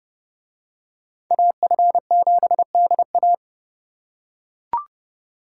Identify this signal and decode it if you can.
Morse code, heard as a keyed tone, about 28 wpm, 700 Hz. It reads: AF7DA